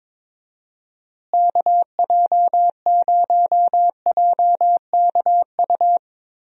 Morse code, heard as a keyed tone, about 22 wpm, 700 Hz. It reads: KJ0JKU
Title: USA Call Signs - 6 Characters - Repeated 2x - 22wpm